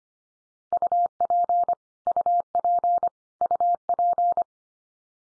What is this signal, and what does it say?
Morse code, heard as a keyed tone, about 25 wpm, 700 Hz. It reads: UP UP UP
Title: Top 100 Words 25wpm x3